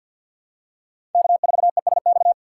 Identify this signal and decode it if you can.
Morse code, heard as a keyed tone, about 49 wpm, 700 Hz. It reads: K4ESX
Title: USA Call Signs - 5 Characters - 3x Character Spacing - 50wpm